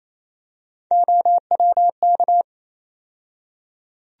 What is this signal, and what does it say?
Morse code, heard as a keyed tone, about 28 wpm, 700 Hz. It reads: OWK
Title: USA Call Sign Postfixes - 3 Letters 28wpm